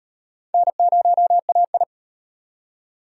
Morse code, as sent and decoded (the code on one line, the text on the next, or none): -. ----- .- ..
N0AI